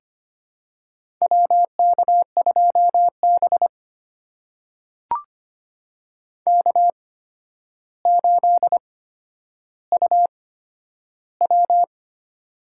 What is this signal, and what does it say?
Morse code, heard as a keyed tone, about 25 wpm, spaced as for 11 wpm, 700 Hz. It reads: WK2B K8UW
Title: USA Call Signs - 4 Characters - 8x Character Spacing - 25wpm